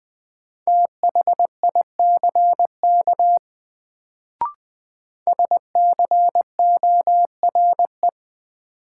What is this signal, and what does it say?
Morse code, heard as a keyed tone, about 20 wpm, 700 Hz. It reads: THICK SCORE
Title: Sentences from Top 1600 Words - New Word Drill - 20wpm